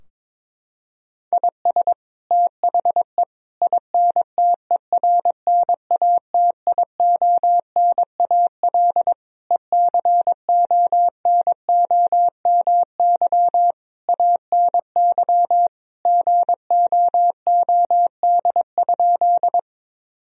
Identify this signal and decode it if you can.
Morse code, heard as a keyed tone, about 21 wpm, 700 Hz. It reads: IS THE INTERNATIONAL ECONOMY ANY GOOD?
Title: Sentences from Top 1000 Words - Review Part 4 - 22wpm